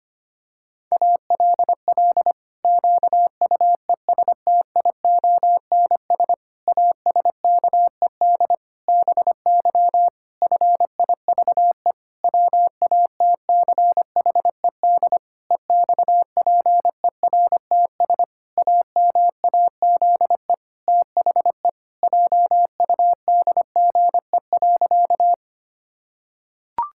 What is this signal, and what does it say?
Morse code, heard as a keyed tone, about 25 wpm, 700 Hz. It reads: ALL QUESTIONS ASKED BY FIVE WATCHED EXPERTS AMAZE THE JUDGE.
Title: Pangram Challenge v2 - 25wpm